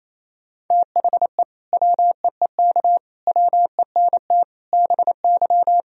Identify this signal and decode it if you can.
Morse code, heard as a keyed tone, about 28 wpm, 700 Hz. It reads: THE WEEK WENT BY
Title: Sentences from Top 1000 Words - Review Part 2 - 28wpm